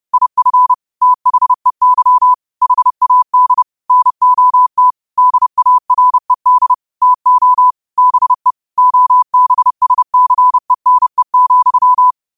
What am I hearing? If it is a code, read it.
Morse code, heard as a keyed tone, about 30 wpm, 1000 Hz. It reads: IF THEY HAD NOT DARED TO BE OBSCENE,
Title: 30 WPM morse code quotes for Fri, 01 Mar 2024